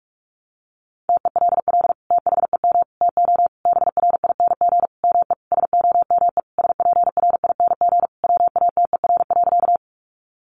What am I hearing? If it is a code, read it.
Morse code, heard as a keyed tone, about 44 wpm, 700 Hz. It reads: TELL THEM TO BRING ME SOME SPRING WATER.
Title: Sentences from Top 1200 Words 45wpm